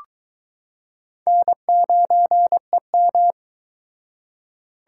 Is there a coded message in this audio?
Yes: Morse code from 1.3 to 3.3 s, heard as a keyed tone, about 22 wpm, 700 Hz. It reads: N9EM